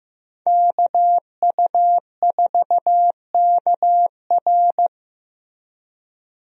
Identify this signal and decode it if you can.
Morse code, heard as a keyed tone, about 15 wpm, 700 Hz. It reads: KU4KR
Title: USA Call Signs - 5 Characters - Repeated 3x - 15wpm